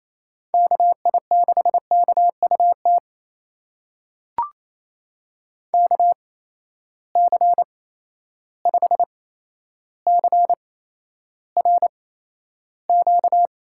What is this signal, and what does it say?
Morse code, heard as a keyed tone, about 28 wpm, spaced as for 12 wpm, 700 Hz. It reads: KI6KUT KC5CRQ
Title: USA Call Signs - 6 Characters - 8x Character Spacing - 28wpm